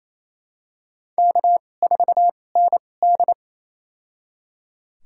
Morse code, heard as a keyed tone, about 28 wpm, spaced as for 20 wpm, 700 Hz. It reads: K4ND
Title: USA Call Signs - 2x Character Spacing - 28wpm